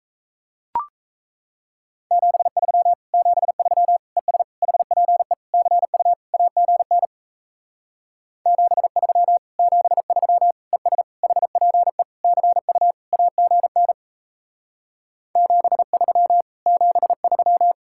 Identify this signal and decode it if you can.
Morse code, heard as a keyed tone, about 38 wpm, 700 Hz. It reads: 73 73 ES HPE CU AGN 73 73 ES HPE CU AGN 73 73